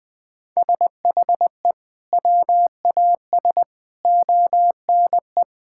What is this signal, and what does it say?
Morse code, heard as a keyed tone, about 21 wpm, 700 Hz. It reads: SHE WAS ONE
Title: Sentences with 6-Word Phrases - 20wpm